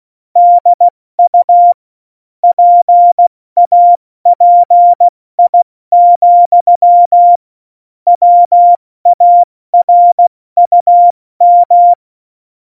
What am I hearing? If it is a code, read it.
Morse code, heard as a keyed tone, about 16 wpm, 700 Hz. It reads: DU PAPI, WARUM
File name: Morsekurs / Witze / Witz 06. November 2018, 70 F80